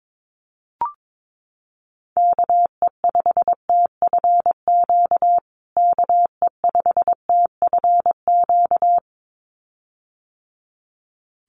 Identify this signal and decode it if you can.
Morse code, heard as a keyed tone, about 22 wpm, 700 Hz. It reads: KE5TFQ KE5TFQ
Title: USA Call Signs Repeated 2x 22wpm